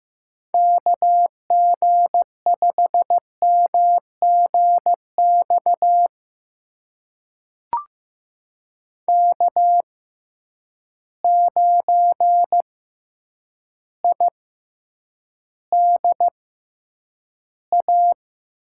Morse code, heard as a keyed tone, about 15 wpm, spaced as for 8 wpm, 700 Hz. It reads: KG5MGX K9IDA
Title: USA Call Signs - 6x Character Spacing - 15wpm